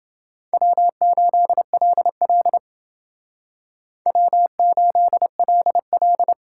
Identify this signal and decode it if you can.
Morse code, heard as a keyed tone, about 29 wpm, 700 Hz. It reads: W8LL W8LL